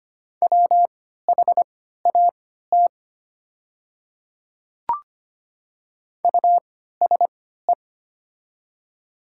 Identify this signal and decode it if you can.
Morse code, heard as a keyed tone, about 25 wpm, spaced as for 14 wpm, 700 Hz. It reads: WHAT USE
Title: 3x Character Spacing - Top 100 Words - 25wpm